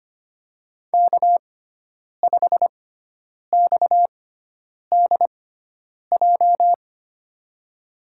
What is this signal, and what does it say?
Morse code, heard as a keyed tone, about 25 wpm, spaced as for 9 wpm, 700 Hz. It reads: K5XDJ
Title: USA Call Signs - 5 Characters - 6x Character Spacing - 25wpm